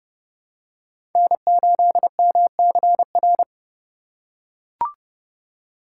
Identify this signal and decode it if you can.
Morse code, heard as a keyed tone, about 32 wpm, 700 Hz. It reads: N8MCR
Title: USA Call Signs - 5 Characters - 4x Character Spacing - 30wpm